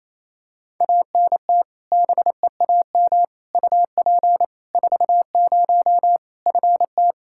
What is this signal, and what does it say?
Morse code, heard as a keyed tone, about 28 wpm, 700 Hz. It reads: ANT BEAM UP 40 FT
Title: Practice QSO - Exchange 2 - Info - 28wpm